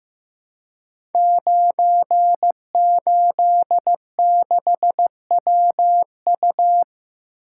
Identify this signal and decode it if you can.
Morse code, heard as a keyed tone, about 15 wpm, 700 Hz. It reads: 986WU
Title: Code Groups - 5 Letter:Numbers - 15wpm